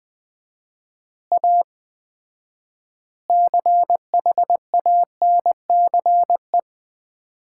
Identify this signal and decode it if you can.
Morse code, heard as a keyed tone, about 21 wpm, 700 Hz. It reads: A CHANCE
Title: 4x Word-Spacing - Sets of 4 Words - 20wpm